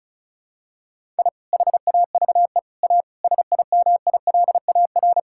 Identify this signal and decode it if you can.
Morse code, heard as a keyed tone, about 36 wpm, 700 Hz. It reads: I HAVE A SIMILAR
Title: Sentences from Top 1000 Words - Review Part 5 - 35wpm